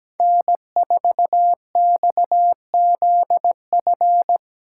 Morse code, heard as a keyed tone, about 20 wpm, 700 Hz. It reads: N4XZF
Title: Speed Racing - USA 1x3 Call Signs 17wpm